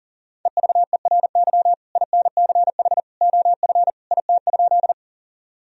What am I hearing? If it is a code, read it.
Morse code, heard as a keyed tone, about 40 wpm, 700 Hz. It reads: EVERY INCH OF IT?
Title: Sentences from Top 1000 Words - Review Part 3 - 40wpm